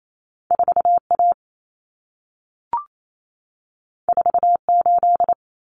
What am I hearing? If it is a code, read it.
Morse code, heard as a keyed tone, about 28 wpm, 700 Hz. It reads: 4A 48
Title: Common Cut Numbers - 2-Digit 28wpm